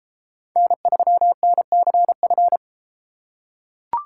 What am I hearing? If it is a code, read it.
Morse code, heard as a keyed tone, about 33 wpm, 700 Hz. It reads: N3NCF